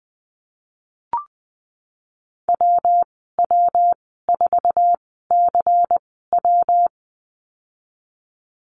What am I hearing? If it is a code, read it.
Morse code, heard as a keyed tone, about 19 wpm, spaced as for 14 wpm, 700 Hz. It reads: WW4CW
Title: 122 Lesson 20g - Number 3 - Call Signs 20wpm